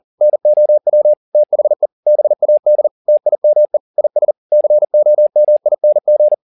Morse code, heard as a keyed tone, about 40 wpm, 600 Hz. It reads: NOW THE BAD TIME IS COMING
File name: the old man and the sea 2 40wpm